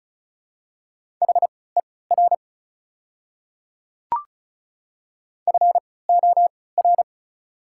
Morse code, heard as a keyed tone, about 36 wpm, spaced as for 20 wpm, 690 Hz. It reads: HER FOR
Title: 3x Character Spacing - Top 100 Words - 35wpm